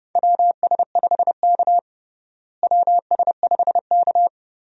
Morse code, heard as a keyed tone, about 32 wpm, 700 Hz. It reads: WS5K WS5K
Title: USA Call Signs - 4 Characters - Repeated 2x - 30wpm